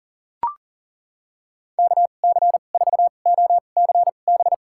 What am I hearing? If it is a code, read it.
Morse code, heard as a keyed tone, about 40 wpm, spaced as for 29 wpm, 700 Hz. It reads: KC4OCB
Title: USA Call Signs - 6 Characters - 2x Character Spacing - 40wpm